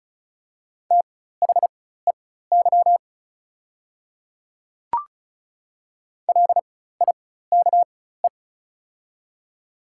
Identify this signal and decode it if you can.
Morse code, heard as a keyed tone, about 36 wpm, spaced as for 16 wpm, 700 Hz. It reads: THEY LIKE
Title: 4x Character Spacing - Top 200 Words - 35wpm